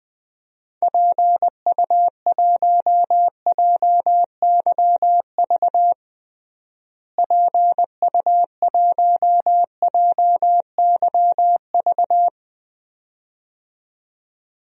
Morse code, heard as a keyed tone, about 20 wpm, 690 Hz. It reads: PU1JYV PU1JYV
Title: DXCC-100 Call Signs - 6 Characters - Repeated 2x - 20wpm